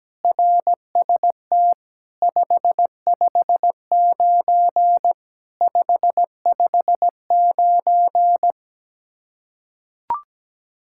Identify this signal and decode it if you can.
Morse code, heard as a keyed tone, about 17 wpm, 700 Hz. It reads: RST 559 559
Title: Practice QSO - Exchange 1 - Beginning - 17wpm